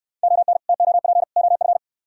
Morse code, heard as a keyed tone, about 63 wpm, 700 Hz. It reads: ON TOP OF